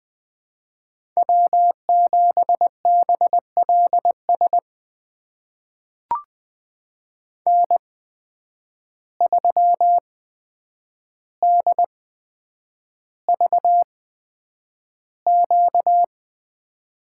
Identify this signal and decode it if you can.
Morse code, heard as a keyed tone, about 21 wpm, spaced as for 9 wpm, 700 Hz. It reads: W7BLS N3DVQ
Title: USA Call Signs - 5 Characters - 8x Character Spacing - 20wpm